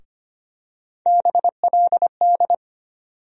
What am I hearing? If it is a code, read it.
Morse code, heard as a keyed tone, about 25 wpm, 700 Hz. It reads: BLD
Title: USA Call Sign Postfixes - 3 Letters 25wpm